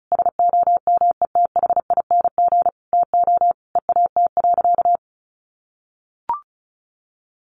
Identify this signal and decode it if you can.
Morse code, heard as a keyed tone, about 34 wpm, 700 Hz. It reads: SOMETHING TO EAT.
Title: Sentences from Top 600 Words 35wpm